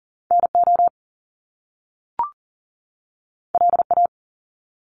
Morse code, heard as a keyed tone, about 40 wpm, 700 Hz. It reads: NO LA
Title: Top Two Letter Combinations in Words 40wpm